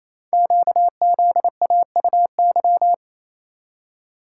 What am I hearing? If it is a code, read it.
Morse code, heard as a keyed tone, about 27 wpm, 700 Hz. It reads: QZAUY
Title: Code Groups - 5 Letters - 28wpm